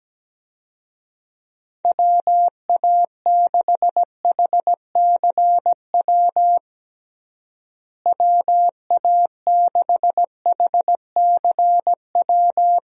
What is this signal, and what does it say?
Morse code, heard as a keyed tone, about 17 wpm, 700 Hz. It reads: WA6HCW WA6HCW
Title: USA Call Signs - 6 Characters - Repeated 2x - 17wpm